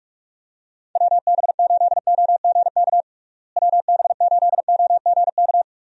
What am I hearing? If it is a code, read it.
Morse code, heard as a keyed tone, about 46 wpm, 700 Hz. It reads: WB8OGK WB8OGK
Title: USA Call Signs - 6 Characters - Repeated 3x - 45wpm